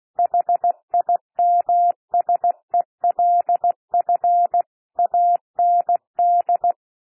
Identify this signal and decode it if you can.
Morse code, heard as a keyed tone, about 16 wpm, 700 Hz. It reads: HIMSELF AND
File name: cwmp3 / P1 SOUND PASSAGE 16 wpm